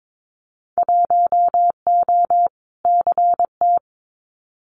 Morse code, heard as a keyed tone, about 22 wpm, 700 Hz. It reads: 1O CT
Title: Winter Field Day Exchanges 22wpm